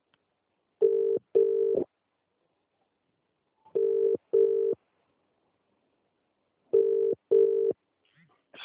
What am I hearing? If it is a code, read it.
Ringback tone (UK), 400 + 450 Hz.